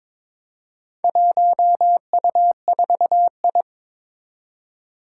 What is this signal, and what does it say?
Morse code, heard as a keyed tone, about 21 wpm, 700 Hz. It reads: 1U4I
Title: Four Challenging Letter-Number Combinations 22wpm